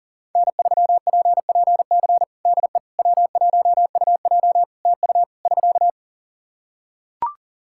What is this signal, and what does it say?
Morse code, heard as a keyed tone, about 40 wpm, 700 Hz. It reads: N3PPC DE W1UJ TU <SK>
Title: Practice QSO - Exchange 4 - Ending - 40wpm